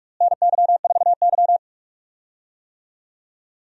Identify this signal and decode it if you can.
Morse code, heard as a keyed tone, about 46 wpm, 700 Hz. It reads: NY4Y
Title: USA Call Signs - 4 Characters - Repeated 2x - 45wpm